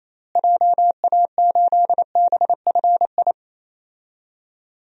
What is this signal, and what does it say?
Morse code, heard as a keyed tone, about 28 wpm, 700 Hz. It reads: JA8BFI
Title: DXCC-100 Call Signs - 6 Characters - Repeated 3x - 28wpm